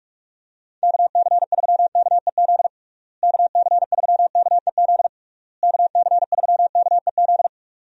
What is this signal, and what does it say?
Morse code, heard as a keyed tone, about 47 wpm, 700 Hz. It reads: KC3KEZ KC3KEZ KC3KEZ
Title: USA Call Signs - 6 Characters - Repeated 3x - 45wpm